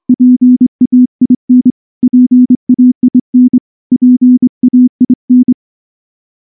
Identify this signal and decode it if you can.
Morse code, heard as a keyed tone, about 24 wpm, 250 Hz. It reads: PAIN PAIN PAIN
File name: morse-training-25WPM